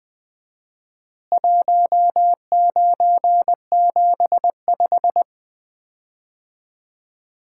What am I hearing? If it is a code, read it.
Morse code, heard as a keyed tone, about 21 wpm, 700 Hz. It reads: 1975
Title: Common QSO Phrases - 4x Word-Spacing - 20wpm